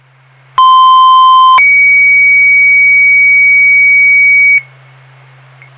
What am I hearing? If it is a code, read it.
Two-tone page, heard as a one-second tone, then a three-second tone, about 1040 then 2250 Hz, with noise. It is heard from 0.6 to 4.6 s.